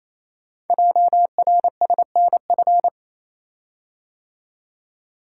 Morse code, heard as a keyed tone, about 29 wpm, 700 Hz. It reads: JRSNF
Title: Code Groups - 5 Letters - 28wpm